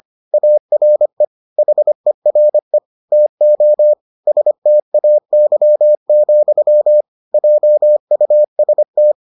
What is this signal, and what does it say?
Morse code, heard as a keyed tone, about 25 wpm, 600 Hz. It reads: ARE HERE TO STAY, JUST